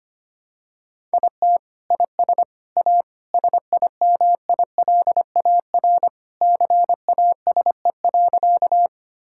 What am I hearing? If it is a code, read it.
Morse code, heard as a keyed tone, about 25 wpm, 700 Hz. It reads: IT IS A SIMILAR CASE.
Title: Sentences from Top 1000 Words - Review Part 5 - 25wpm